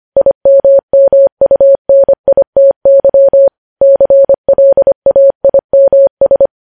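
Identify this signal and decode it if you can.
Morse code, heard as a keyed tone, about 25 wpm, 550 Hz. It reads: IMMUNITY CLAIMS
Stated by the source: Morse news headlines sent at 25 wpm.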